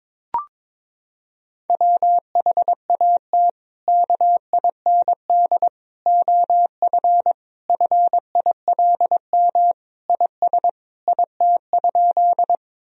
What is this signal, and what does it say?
Morse code, heard as a keyed tone, about 22 wpm, 700 Hz. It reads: WHAT KIND OF FILM IS IT?
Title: Sentences from Top 1000 Words - Review Part 3 - 22wpm